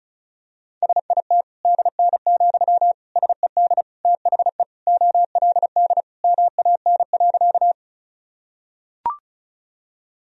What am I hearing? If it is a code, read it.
Morse code, heard as a keyed tone, about 36 wpm, 700 Hz. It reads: SIT DN, SED THE OLD MAN.
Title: Sentences from Top 500 Words and Abbreviations 35wpm